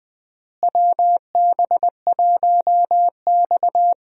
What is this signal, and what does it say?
Morse code, heard as a keyed tone, about 20 wpm, 700 Hz. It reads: WB1X
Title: USA Call Signs - 4 Characters - Repeated 2x - 20wpm